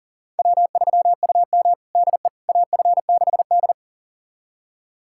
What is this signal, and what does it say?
Morse code, heard as a keyed tone, about 37 wpm, 690 Hz. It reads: W3UM DE AF6D
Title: USA Callsign DE Callsign - 4 Characters - 40wpm